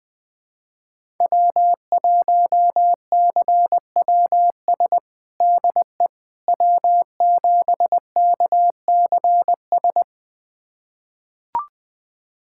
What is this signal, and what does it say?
Morse code, heard as a keyed tone, about 20 wpm, 700 Hz. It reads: W1CWS DE W7KCS
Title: USA Callsign DE Callsign - 5 Characters - 20wpm